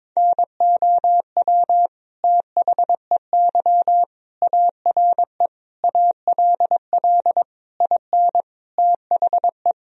Morse code, heard as a keyed tone, about 22 wpm, 700 Hz. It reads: NOW THEY ARE ALL IN THE
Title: Sentences from Top 1000 Words - Review Part 2 - 22wpm